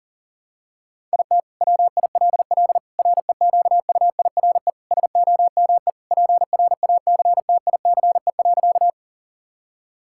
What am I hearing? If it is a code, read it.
Morse code, heard as a keyed tone, about 40 wpm, 700 Hz. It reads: IT WILL REQUIRE SOME PRACTICE.
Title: Sentences from Top 1000 Words - Review Part 6 - 40wpm